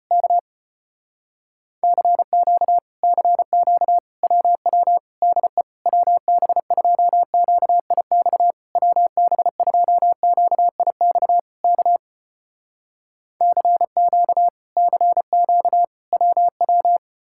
Morse code, heard as a keyed tone, about 34 wpm, 700 Hz. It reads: K CQ CQ WW DE WB2QIX WB2QIX K CQ CQ WW